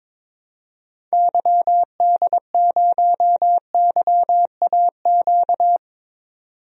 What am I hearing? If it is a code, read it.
Morse code, heard as a keyed tone, about 21 wpm, 700 Hz. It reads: YD0YAQ
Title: DXCC-100 Call Signs - 6 Characters - 22wpm